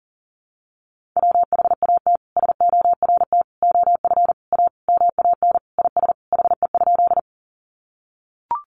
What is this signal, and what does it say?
Morse code, heard as a keyed tone, about 40 wpm, 700 Hz. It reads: WHAT SORT OF A MAN IS HE?
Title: Sentences from Top 1300 Words 40wpm